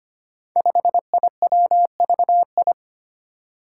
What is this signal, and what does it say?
Morse code, heard as a keyed tone, about 25 wpm, 700 Hz. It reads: 5IWVI
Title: Five Challenging Letter-Number Combinations 25wpm